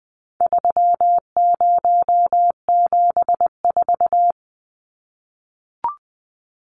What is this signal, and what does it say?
Morse code, heard as a keyed tone, about 20 wpm, 700 Hz. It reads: 3074
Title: Numbers - 4-Digit 20wpm